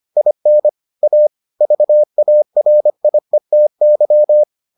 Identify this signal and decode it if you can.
Morse code, heard as a keyed tone, about 25 wpm, 600 Hz. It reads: IN A VARIETY